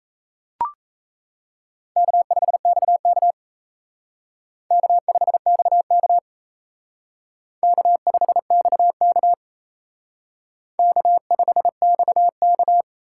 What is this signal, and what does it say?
Morse code, heard as a keyed tone, about 35 wpm, 700 Hz. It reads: K5XK K5XK K5XK K5XK